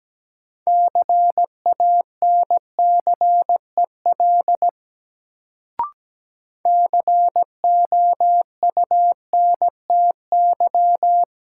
Morse code, heard as a keyed tone, about 17 wpm, 700 Hz. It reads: CANCEL COUNTY
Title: Sentences from Top 1200 Words - New Word Drill - 17wpm